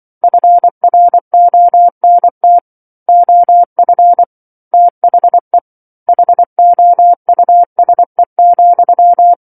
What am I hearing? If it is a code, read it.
Morse code, heard as a keyed tone, about 24 wpm, 700 Hz. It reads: FRONT OF THE HOUSE,
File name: 07-Chapter-07 24 WPM